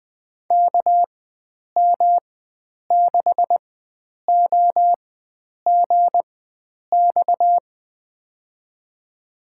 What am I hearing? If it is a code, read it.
Morse code, heard as a keyed tone, about 21 wpm, spaced as for 9 wpm, 700 Hz. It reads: KM6OGX
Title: USA Call Signs - 6 Characters - 4x Character Spacing - 20wpm